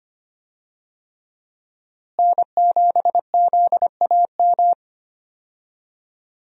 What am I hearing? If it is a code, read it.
Morse code, heard as a keyed tone, about 25 wpm, 700 Hz. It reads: N7ZAM
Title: USA Call Signs - 5 Characters - Repeated 3x - 25wpm